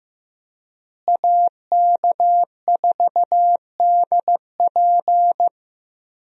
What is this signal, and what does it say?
Morse code, heard as a keyed tone, about 15 wpm, 700 Hz. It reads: AK4DP
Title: USA Call Signs - 5 Characters - Repeated 3x - 15wpm